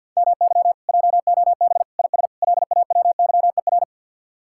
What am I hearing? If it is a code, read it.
Morse code, heard as a keyed tone, about 64 wpm, 700 Hz. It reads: MY JOB IS LAWYER